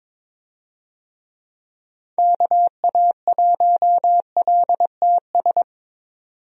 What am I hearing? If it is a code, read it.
Morse code, heard as a keyed tone, about 23 wpm, 700 Hz. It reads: KA1LTS
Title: USA Call Signs - 6 Characters - 22wpm